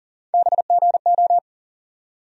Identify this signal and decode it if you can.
Morse code, heard as a keyed tone, about 40 wpm, 700 Hz. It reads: BGO
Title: USA Call Sign Postfixes - 3 Letters 40wpm